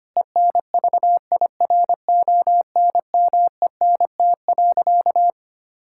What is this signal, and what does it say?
Morse code, heard as a keyed tone, about 25 wpm, 700 Hz. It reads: ENVIRONMENT.
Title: Sentences from Top 1000 Words - Review Part 5 - 25wpm